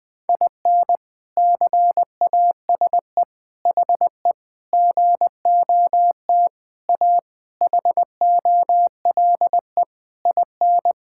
Morse code, heard as a keyed tone, about 21 wpm, 700 Hz. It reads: IN CASE HE GOT A HOLE IN
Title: All-Age Appropriate Jokes - 20wpm